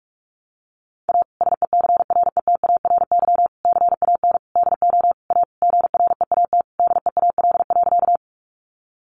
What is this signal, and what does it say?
Morse code, heard as a keyed tone, about 45 wpm, 700 Hz. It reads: A SECRETARY CAN DO A GREAT DEAL.
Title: Sentences from Top 1300 Words 45wpm